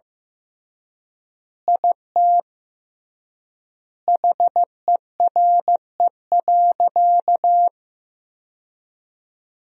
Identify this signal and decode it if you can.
Morse code, heard as a keyed tone, about 15 wpm, 700 Hz. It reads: IT HERE.
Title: 3x Word-Spacing - Sentences from Top 500 Words - 15wpm